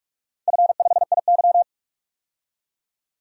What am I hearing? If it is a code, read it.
Morse code, heard as a keyed tone, about 45 wpm, 700 Hz. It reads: F5IY
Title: DXCC-100 Call Signs - 4 Characters - 45wpm